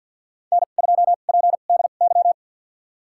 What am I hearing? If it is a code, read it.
Morse code, heard as a keyed tone, about 51 wpm, spaced as for 36 wpm, 700 Hz. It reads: N2PDY
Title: USA Call Signs - 5 Characters - 2x Character Spacing - 50wpm